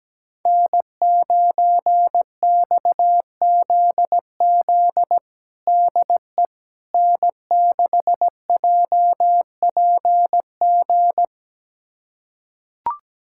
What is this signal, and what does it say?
Morse code, heard as a keyed tone, about 17 wpm, 700 Hz. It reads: N9XZZ DE N6JPG
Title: USA Callsign DE Callsign - 5 Characters - 17wpm